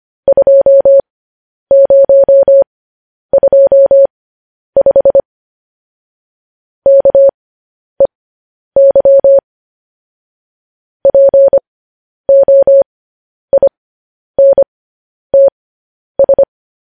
Morse code, heard as a keyed tone, about 25 wpm, spaced as for 10 wpm, 550 Hz. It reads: 2025 KEY POINTS
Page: news-headlines-in-morse-code-at-10-wpm